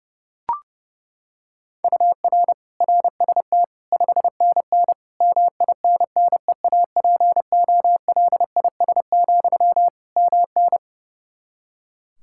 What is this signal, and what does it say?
Morse code, heard as a keyed tone, about 30 wpm, 700 Hz. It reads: UR RST 5NN MINNEAPOLIS, MN
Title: Practice QSO - Exchange 1 - Beginning - 30wpm